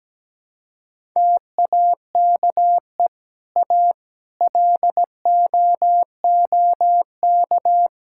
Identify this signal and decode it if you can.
Morse code, heard as a keyed tone, about 17 wpm, 700 Hz. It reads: TAKE A LOOK
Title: Sentences from Top 1000 Words - Review Part 5 - 17wpm